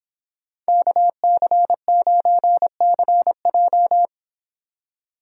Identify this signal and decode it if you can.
Morse code, heard as a keyed tone, about 26 wpm, 700 Hz. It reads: KC9CJ